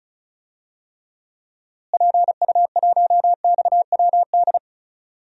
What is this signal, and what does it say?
Morse code, heard as a keyed tone, about 34 wpm, 700 Hz. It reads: PU1XWD
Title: DXCC-100 Call Signs - 6 Characters - Repeated 2x - 35wpm